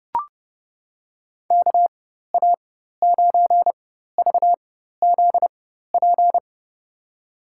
Morse code, heard as a keyed tone, about 32 wpm, spaced as for 14 wpm, 700 Hz. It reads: KA9VZP
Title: USA Call Signs - 6 Characters - 4x Character Spacing - 30wpm